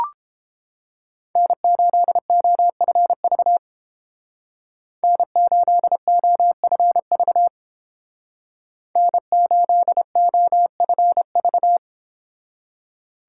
Morse code, heard as a keyed tone, about 29 wpm, 700 Hz. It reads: N8OFV N8OFV N8OFV